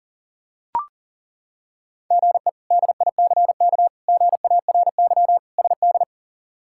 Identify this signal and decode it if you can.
Morse code, heard as a keyed tone, about 40 wpm, 700 Hz. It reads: GE DICK GARY SD
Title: K1USN SST QSO Practice - 40wpm